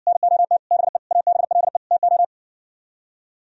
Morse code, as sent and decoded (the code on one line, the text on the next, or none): -. --- - / -... / .- -... .-.. / - ---
NOT BE ABLE TO